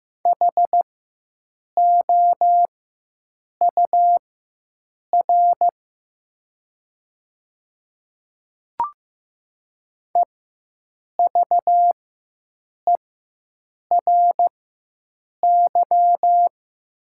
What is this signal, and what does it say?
Morse code, heard as a keyed tone, about 15 wpm, spaced as for 7 wpm, 700 Hz. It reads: HOUR EVERY
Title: 4x Character Spacing - Top 500 Words - 15wpm